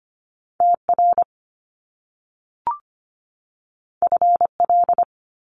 Morse code, heard as a keyed tone, about 25 wpm, 700 Hz. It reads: T R F L
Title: Top Two Letter Combinations in Words 25wpm